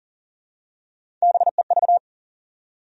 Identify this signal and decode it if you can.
Morse code, heard as a keyed tone, about 40 wpm, 700 Hz. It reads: BEV
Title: USA Common Names - Max 5 Letters - 40wpm